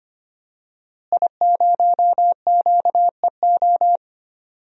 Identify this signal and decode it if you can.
Morse code, heard as a keyed tone, about 25 wpm, 700 Hz. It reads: I0QEO
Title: DXCC-100 Call Signs - 5 Characters - Repeated 3x - 25wpm